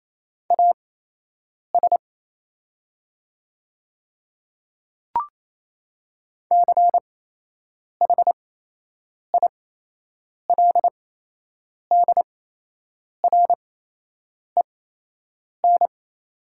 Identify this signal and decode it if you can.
Morse code, heard as a keyed tone, about 28 wpm, spaced as for 8 wpm, 690 Hz. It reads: AS CHILDREN
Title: 8x Character Spacing - Top 500 Words - 28wpm